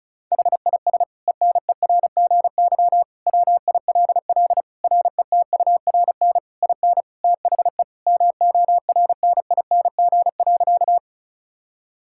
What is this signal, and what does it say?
Morse code, heard as a keyed tone, about 36 wpm, 690 Hz. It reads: HIS ENERGY WILL RETURN IN THE MORNING.
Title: Sentences from Top 1000 Words - Review Part 6 - 35wpm